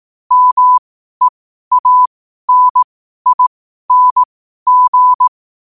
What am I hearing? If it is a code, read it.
Morse code, heard as a keyed tone, about 18 wpm, spaced as for 12 wpm, 1000 Hz. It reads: MEANING